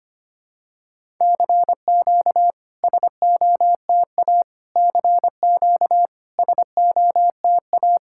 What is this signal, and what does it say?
Morse code, heard as a keyed tone, about 25 wpm, 700 Hz. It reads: CQ SOTA CQ SOTA
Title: SOTA QSO Practice 25wpm